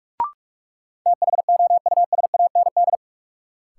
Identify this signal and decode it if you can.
Morse code, heard as a keyed tone, about 46 wpm, 700 Hz. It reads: THOUSAND
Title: Sentences from Top 600 Words - New Word Drill - 45wpm